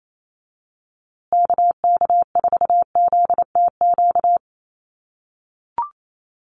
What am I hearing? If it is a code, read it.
Morse code, heard as a keyed tone, about 28 wpm, 700 Hz. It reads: KK4ZTQ
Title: USA Call Signs Repeated 3x 28wpm